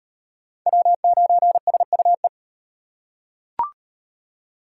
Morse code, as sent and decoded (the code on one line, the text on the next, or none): .-- ----. ... ..- .
W9SUE